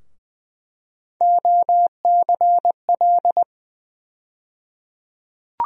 Morse code, heard as a keyed tone, about 20 wpm, 700 Hz. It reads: OCL
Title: USA Call Sign Postfixes - 3 Letters 20wpm